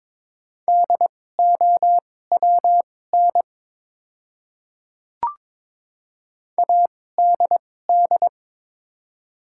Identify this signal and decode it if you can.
Morse code, heard as a keyed tone, about 22 wpm, spaced as for 16 wpm, 690 Hz. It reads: DOWN ADD
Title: 2x Character Spacing - Top 500 Words - 22wpm